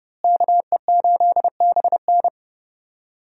Morse code, heard as a keyed tone, about 30 wpm, 700 Hz. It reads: KE8BN
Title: USA Call Signs - 5 Characters - Repeated 3x - 30wpm